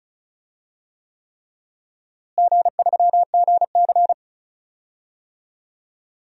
Morse code, heard as a keyed tone, about 34 wpm, 700 Hz. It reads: G3GC
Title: DXCC-100 Call Signs - 4 Characters - Repeated 3x - 35wpm